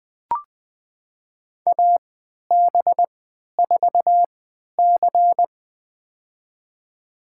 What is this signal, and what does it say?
Morse code, heard as a keyed tone, about 20 wpm, spaced as for 11 wpm, 700 Hz. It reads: AB4C
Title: USA Call Signs - 4 Characters - 3x Character Spacing - 20wpm